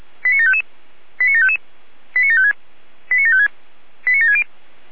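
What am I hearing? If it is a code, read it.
Five-tone selective call (ZVEI), heard as quick runs of five tones, tones of about 70 ms. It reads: 89755 89755 89756 89756 89760
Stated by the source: ZVEI five-tone call received by SDR.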